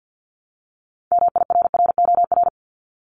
Morse code, heard as a keyed tone, about 50 wpm, 690 Hz. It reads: MIRROR
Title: Multi-letter Phonemes 50wpm